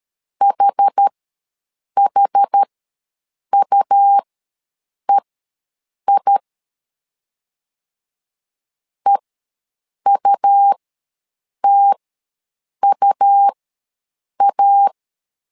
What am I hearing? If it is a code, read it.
Morse code, heard as a keyed tone, about 13 wpm, spaced as for 7 wpm, 800 Hz. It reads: HHUEI EUTUA
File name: LEZIONI / 2 UHDN / 2 UH ws 12wpm 1730